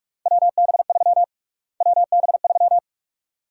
Morse code, heard as a keyed tone, about 42 wpm, 700 Hz. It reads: WB3 WB3
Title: USA Call Sign Prefixes - Repeated 2x - 45wpm